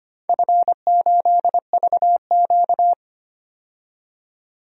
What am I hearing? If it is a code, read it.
Morse code, heard as a keyed tone, about 25 wpm, 700 Hz. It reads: F8VQ
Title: DXCC-100 Call Signs - 4 Characters - 25wpm